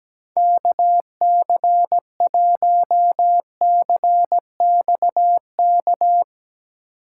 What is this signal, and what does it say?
Morse code, heard as a keyed tone, about 17 wpm, 700 Hz. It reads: KC1CXK
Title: USA Call Signs - 6 Characters - 6x Character Spacing - 17wpm